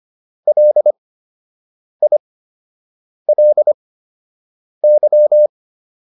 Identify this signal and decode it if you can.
Morse code, heard as a keyed tone, about 25 wpm, spaced as for 7 wpm, 600 Hz. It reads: LILY